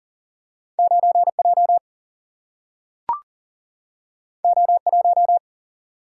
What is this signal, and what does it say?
Morse code, heard as a keyed tone, about 40 wpm, 700 Hz. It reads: 9J O1
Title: Single-Letter Single-Number 40wpm